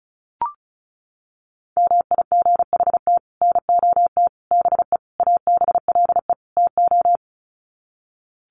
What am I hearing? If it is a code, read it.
Morse code, heard as a keyed tone, about 34 wpm, 700 Hz. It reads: MIGHT NOT BE ABLE TO
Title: Sets of 5 Words Encore2 35wpm